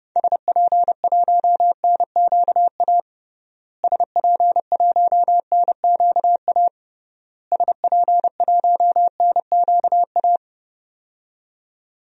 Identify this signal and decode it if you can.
Morse code, heard as a keyed tone, about 32 wpm, 700 Hz. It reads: SP1NQA SP1NQA SP1NQA
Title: DXCC-100 Call Signs - 6 Characters - Repeated 3x - 30wpm